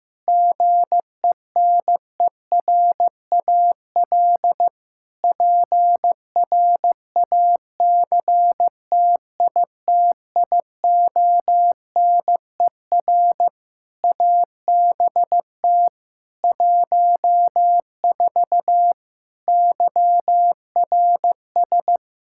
Morse code, heard as a keyed tone, about 15 wpm, 700 Hz. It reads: GENERAL PRACTITIONER ABT 14 YRS
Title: Practice QSO - Exchange 3 - More Info - 15wpm